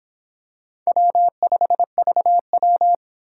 Morse code, heard as a keyed tone, about 26 wpm, 700 Hz. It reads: W5VW